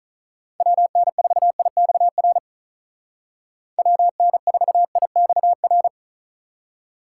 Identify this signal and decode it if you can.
Morse code, heard as a keyed tone, about 47 wpm, 700 Hz. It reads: WN4IXR WN4IXR